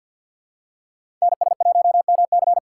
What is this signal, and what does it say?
Morse code, heard as a keyed tone, about 50 wpm, 700 Hz. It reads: NS1MC
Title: USA Call Signs - 5 Characters - 6x Character Spacing - 50wpm